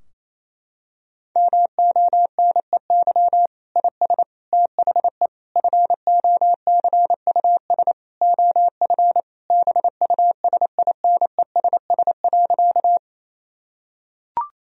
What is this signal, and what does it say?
Morse code, heard as a keyed tone, about 28 wpm, 700 Hz. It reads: MONEY IS THE FOCUS OF BUSINESS.
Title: Sentences from Top 1000 Words - Review Part 4 - 28wpm